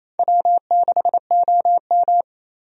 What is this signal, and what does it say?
Morse code, heard as a keyed tone, about 28 wpm, 700 Hz. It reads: W6OM
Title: USA Call Signs - 4 Characters - 28wpm